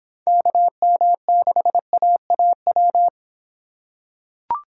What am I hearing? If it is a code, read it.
Morse code, heard as a keyed tone, about 25 wpm, 700 Hz. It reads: KM6AAW